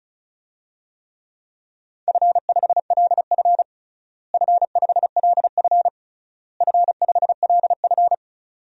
Morse code, heard as a keyed tone, about 36 wpm, 700 Hz. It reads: F5LF F5LF F5LF
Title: DXCC-100 Call Signs - 4 Characters - Repeated 3x - 35wpm